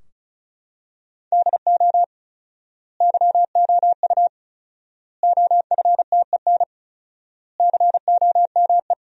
Morse code, heard as a keyed tone, about 34 wpm, 700 Hz. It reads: DO YOU OFTEN COME
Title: 4x Word-Spacing - Sentences from Top 300 Words - 35wpm